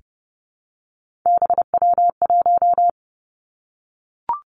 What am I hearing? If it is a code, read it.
Morse code, heard as a keyed tone, about 30 wpm, 700 Hz. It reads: BW1
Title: Three Challenging Letter-Number Combinations 30wpm